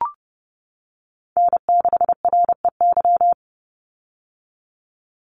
Morse code, heard as a keyed tone, about 30 wpm, 700 Hz. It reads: N6REY
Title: USA Call Signs 30wpm